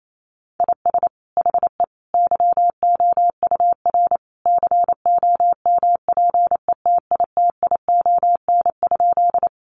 Morse code, heard as a keyed tone, about 28 wpm, 700 Hz. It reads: IS HE YOUR COMPETITION?
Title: Sentences from Top 1000 Words 28wpm